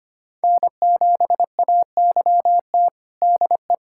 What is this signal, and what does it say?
Morse code, heard as a keyed tone, about 25 wpm, 700 Hz. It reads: N7AYT DE
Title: USA Callsign DE Callsign - 5 Characters - 25wpm